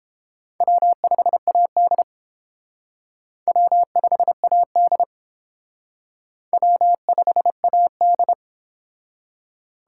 Morse code, heard as a keyed tone, about 30 wpm, 700 Hz. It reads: W5AD W5AD W5AD